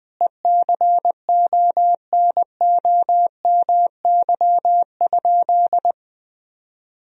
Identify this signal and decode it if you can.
Morse code, heard as a keyed tone, about 20 wpm, 700 Hz. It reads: ECONOMY?
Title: Sentences from Top 1000 Words - Review Part 3 - 20wpm